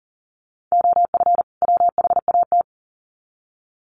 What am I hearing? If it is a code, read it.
Morse code, heard as a keyed tone, about 40 wpm, 700 Hz. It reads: OF WHAT
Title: Sets of 2 Words 40wpm